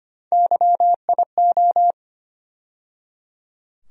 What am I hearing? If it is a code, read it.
Morse code, heard as a keyed tone, about 25 wpm, 700 Hz. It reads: YIO
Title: USA Call Sign Postfixes - 3 Letters 25wpm